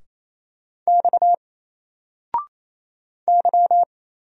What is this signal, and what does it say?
Morse code, heard as a keyed tone, about 28 wpm, 700 Hz. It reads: X Y
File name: Single-Letter-Number-28wpm